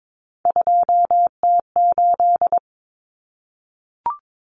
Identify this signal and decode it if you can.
Morse code, heard as a keyed tone, about 22 wpm, 700 Hz. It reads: 2T8
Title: Common Cut Numbers - 3-Digit 22wpm